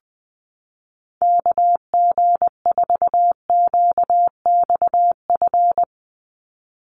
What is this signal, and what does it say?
Morse code, heard as a keyed tone, about 20 wpm, 700 Hz. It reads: KG4QXF
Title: USA Call Signs 20wpm